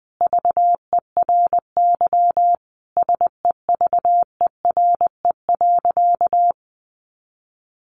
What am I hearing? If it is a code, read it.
Morse code, heard as a keyed tone, about 20 wpm, 700 Hz. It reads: VERY SEVERE.
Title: Sentences from Top 1400 Words 20wpm